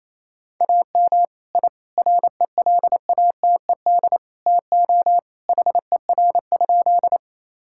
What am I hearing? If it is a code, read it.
Morse code, heard as a keyed tone, about 28 wpm, 700 Hz. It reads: AM I RELATED TO HER?
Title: Sentences from Top 1000 Words - Review Part 5 - 28wpm